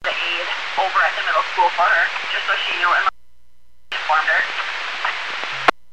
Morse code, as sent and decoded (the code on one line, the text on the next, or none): none